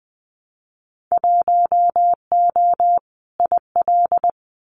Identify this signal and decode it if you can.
Morse code, heard as a keyed tone, about 20 wpm, 700 Hz. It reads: 1O IL
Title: Winter Field Day Exchanges 20wpm